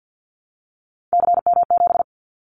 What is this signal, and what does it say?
Morse code, heard as a keyed tone, about 50 wpm, 700 Hz. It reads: CM7
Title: Callsign Prefixes for Top 100 DXCC - 50wpm